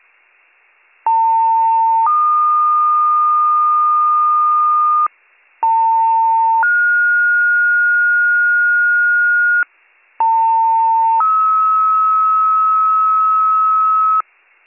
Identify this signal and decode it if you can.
Two-tone page, heard as a one-second tone, then a three-second tone, about 900 then 1220 Hz, about 900 then 1430 Hz, about 900 then 1290 Hz. It is heard from 1.1 to 14.2 s.